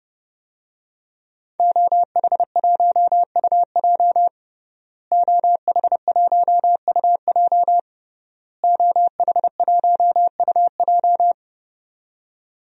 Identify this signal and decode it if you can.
Morse code, heard as a keyed tone, about 30 wpm, 700 Hz. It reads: OH1UJ OH1UJ OH1UJ
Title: DXCC-100 Call Signs - 5 Characters - Repeated 3x - 30wpm